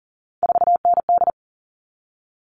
Morse code, heard as a keyed tone, about 40 wpm, 700 Hz. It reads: 4ND
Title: All Cut Numbers - 3-Digit 40wpm